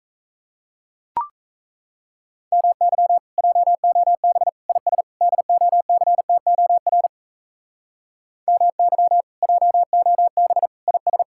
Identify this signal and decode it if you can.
Morse code, heard as a keyed tone, about 34 wpm, 700 Hz. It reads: MY JOB IS DOCTOR MY JOB IS
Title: Speed-Racing Common QSO Phrases - 28wpm